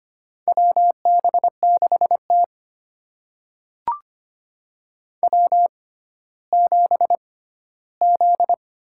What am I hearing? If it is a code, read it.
Morse code, heard as a keyed tone, about 25 wpm, spaced as for 14 wpm, 700 Hz. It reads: WB6T W7Z
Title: USA Call Signs - 4 Characters - 6x Character Spacing - 25wpm